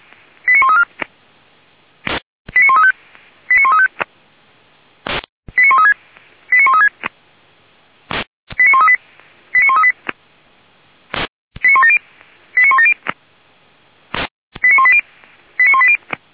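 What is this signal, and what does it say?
Five-tone selective call (ZVEI), heard as quick runs of five tones, tones of about 70 ms. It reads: 89134 89135 89135 89136 89136 89138 89138 89170 89170 89180 89180
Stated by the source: ZVEI five-tone selective call received on an SDR, from a feuerwehr-orlamuende.org fire brigade audio archive.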